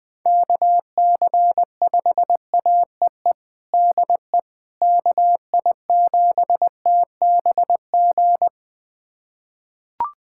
Morse code, heard as a keyed tone, about 21 wpm, 700 Hz. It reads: KC5AEE DE KI7TBG
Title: USA Callsign DE Callsign - 6 Characters - 20wpm